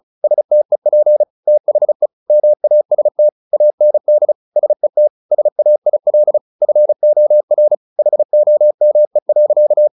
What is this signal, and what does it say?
Morse code, heard as a keyed tone, about 35 wpm, 600 Hz. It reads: STEP THE MAST AND SET SAIL FOR HOME.
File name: the old man and the sea 2 35wpm